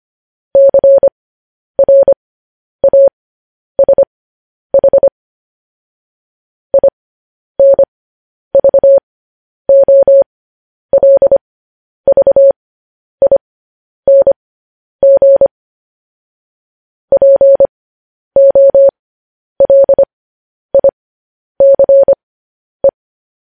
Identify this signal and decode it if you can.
Morse code, heard as a keyed tone, about 25 wpm, spaced as for 10 wpm, 550 Hz. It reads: CRASH INVOLVING POLICE
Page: news headlines in morse code at 10 wpm